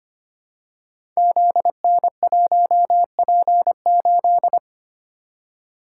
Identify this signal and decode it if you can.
Morse code, heard as a keyed tone, about 25 wpm, 700 Hz. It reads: ZN1P8
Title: Code Groups - 5 Letter:Numbers - 25wpm